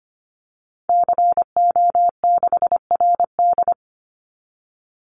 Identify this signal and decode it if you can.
Morse code, heard as a keyed tone, about 25 wpm, 700 Hz. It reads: CO6RD
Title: Call Signs - 25wpm